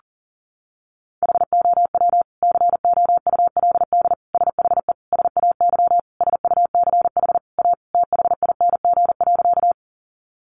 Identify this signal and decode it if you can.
Morse code, heard as a keyed tone, about 40 wpm, 700 Hz. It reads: HOW COULD SHE SAY SUCH A THING.
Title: Sentences from Top 200 Words 40wpm